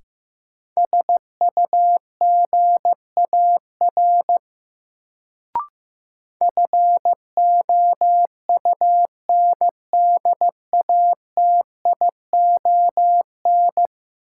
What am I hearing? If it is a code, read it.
Morse code, heard as a keyed tone, about 15 wpm, 700 Hz. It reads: SUGAR FOUNDATION
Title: Sentences from Top 1200 Words - New Word Drill - 15wpm